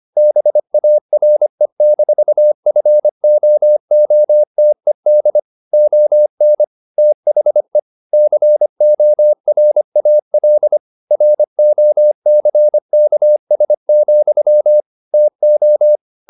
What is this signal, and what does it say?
Morse code, heard as a keyed tone, about 25 wpm, 600 Hz. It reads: BARE-FOOTED ON THE CORAL ROCKS, TO
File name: the old man and the sea part 1 25wpm